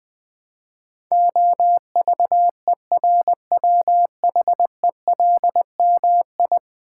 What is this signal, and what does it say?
Morse code, heard as a keyed tone, about 19 wpm, 700 Hz. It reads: OVERWHELMI
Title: Build A Word - 5 Letters At A Time - 12-Letter Words - 20wpm